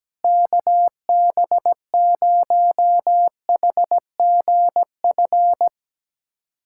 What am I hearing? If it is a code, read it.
Morse code, heard as a keyed tone, about 17 wpm, 700 Hz. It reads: KB0HGF
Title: USA Callsign DE Callsign - 6 Characters - 17wpm